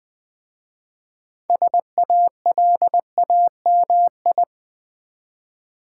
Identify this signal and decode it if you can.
Morse code, heard as a keyed tone, about 20 wpm, 700 Hz. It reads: SALAMI
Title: Easy Words with Hints - 20wpm